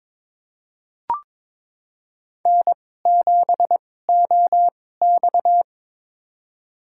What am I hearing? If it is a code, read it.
Morse code, heard as a keyed tone, about 22 wpm, spaced as for 16 wpm, 700 Hz. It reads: N7OX
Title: USA Call Signs - 4 Characters - 2x Character Spacing - 22wpm